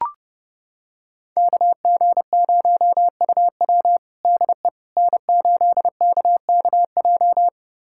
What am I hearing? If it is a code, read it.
Morse code, heard as a keyed tone, about 30 wpm, 700 Hz. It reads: KG0UW DE N8KKJ
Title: USA Callsign DE Callsign - 5 Characters - 30wpm